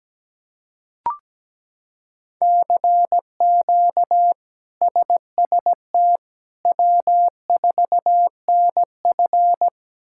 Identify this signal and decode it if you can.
Morse code, heard as a keyed tone, about 17 wpm, 700 Hz. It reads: CQ SST W4NF
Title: K1USN SST QSO Practice - 17wpm